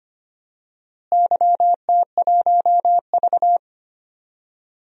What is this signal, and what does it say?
Morse code, heard as a keyed tone, about 25 wpm, 700 Hz. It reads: YT1V
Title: DXCC-100 Call Signs - 4 Characters - Repeated 2x - 25wpm